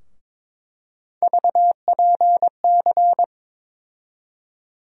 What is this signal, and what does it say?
Morse code, heard as a keyed tone, about 22 wpm, 700 Hz. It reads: VPC
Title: USA Call Sign Postfixes - 3 Letters 22wpm